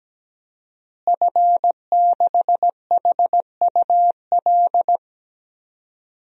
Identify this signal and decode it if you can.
Morse code, heard as a keyed tone, about 17 wpm, 700 Hz. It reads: F6HUL
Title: DXCC-100 Call Signs - 5 Characters - Repeated 2x - 17wpm